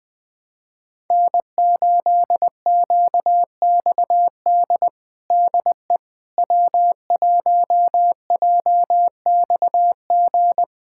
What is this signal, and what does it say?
Morse code, heard as a keyed tone, about 20 wpm, 700 Hz. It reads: N8QXD DE W1JXG
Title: USA Callsign DE Callsign 20wpm